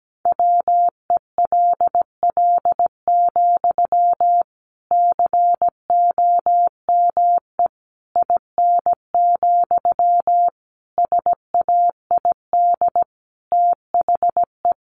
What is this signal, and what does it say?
Morse code, heard as a keyed tone, about 17 wpm, 700 Hz. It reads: WELL, COME IN, SAID THE
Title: Sentences from Top 1000 Words 17wpm